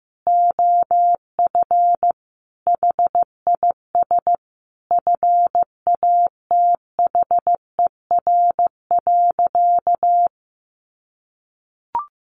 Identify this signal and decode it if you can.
Morse code, heard as a keyed tone, about 15 wpm, 700 Hz. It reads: OF HIS FATHER.
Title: Sentences from Top 1300 Words 15wpm